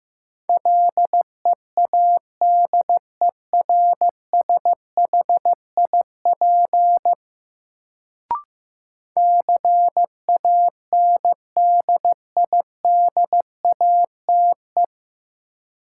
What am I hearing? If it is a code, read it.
Morse code, heard as a keyed tone, about 15 wpm, 700 Hz. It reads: LEADERSHIP CANDIDATE
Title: Sentences from Top 1600 Words - New Word Drill - 15wpm